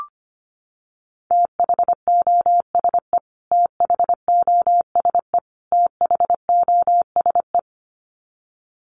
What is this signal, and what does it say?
Morse code, heard as a keyed tone, about 25 wpm, 700 Hz. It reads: THOSE THOSE THOSE
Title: Top 300 Words Repeated 25wpm